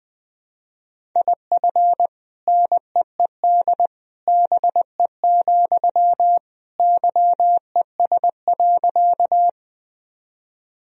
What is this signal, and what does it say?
Morse code, heard as a keyed tone, about 19 wpm, 700 Hz. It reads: IF NEED BE, YES.
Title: Sentences from Top 1000 Words - Review Part 1 - 20wpm